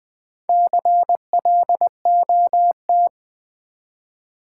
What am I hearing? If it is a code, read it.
Morse code, heard as a keyed tone, about 20 wpm, 700 Hz. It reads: CLOT
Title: Build A Word - 2 Letters At A Time - 8-Letter Words - 20wpm